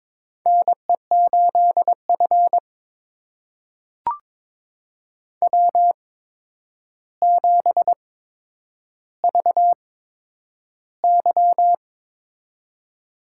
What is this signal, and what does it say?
Morse code, heard as a keyed tone, about 22 wpm, spaced as for 9 wpm, 700 Hz. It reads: NE8F W7VY
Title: USA Call Signs - 4 Characters - 8x Character Spacing - 22wpm